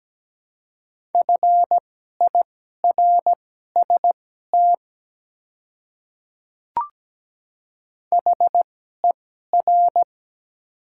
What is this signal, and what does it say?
Morse code, heard as a keyed tone, about 17 wpm, spaced as for 12 wpm, 690 Hz. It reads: FIRST HER
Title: 2x Character Spacing - Top 100 Words - 17wpm